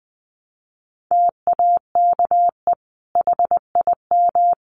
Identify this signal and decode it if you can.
Morse code, heard as a keyed tone, about 20 wpm, 700 Hz. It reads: TAKE HIM
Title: Sentences from Top 600 Words 20wpm